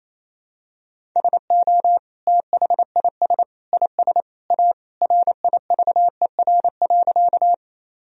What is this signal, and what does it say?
Morse code, heard as a keyed tone, about 28 wpm, 700 Hz. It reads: SO THIS IS A RIVER.
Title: Sentences from Top 1000 Words - Review Part 1 - 28wpm